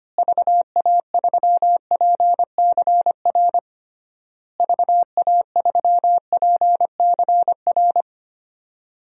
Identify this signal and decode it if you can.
Morse code, heard as a keyed tone, about 25 wpm, 700 Hz. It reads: VA3PCR VA3PCR
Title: DXCC-100 Call Signs - 6 Characters - Repeated 3x - 25wpm